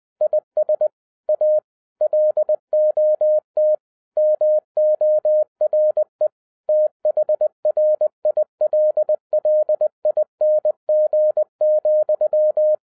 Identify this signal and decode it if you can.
Morse code, heard as a keyed tone, about 20 wpm, 610 Hz. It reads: IS A LOT MORE THRILLING,
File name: 2012-01-mp3 / n16-THROUGH EIGHT MONTHS-20wpm